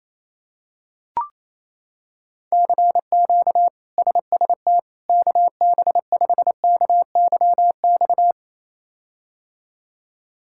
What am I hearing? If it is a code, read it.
Morse code, heard as a keyed tone, about 28 wpm, 700 Hz. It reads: CQ SST KB5KYX
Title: K1USN SST QSO Practice - 28wpm